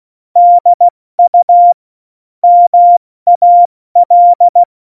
Morse code, heard as a keyed tone, about 16 wpm, 700 Hz. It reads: DU MAL
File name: Morsekurs / Witze / Witz 03. Oktober 17, Tempo 70 F80